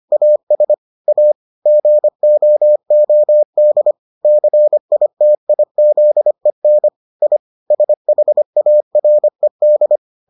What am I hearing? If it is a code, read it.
Morse code, heard as a keyed tone, about 25 wpm, 600 Hz. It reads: AS A GOOD CITIZEN I SHARED